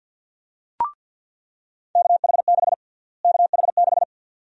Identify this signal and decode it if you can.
Morse code, heard as a keyed tone, about 50 wpm, 700 Hz. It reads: KH6 KH6
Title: USA Call Sign Prefixes - Repeated 2x - 50wpm